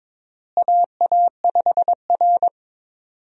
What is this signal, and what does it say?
Morse code, heard as a keyed tone, about 21 wpm, 700 Hz. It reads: AA5R
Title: USA Call Signs - 4 Characters - Repeated 3x - 22wpm